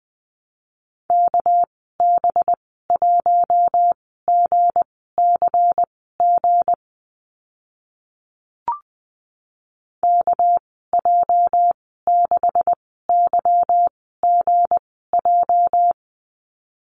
Morse code, heard as a keyed tone, about 20 wpm, spaced as for 14 wpm, 700 Hz. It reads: KB1GCG KJ6YGJ
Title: 219 Lesson 35g - Number 0 - Call Signs 20wpm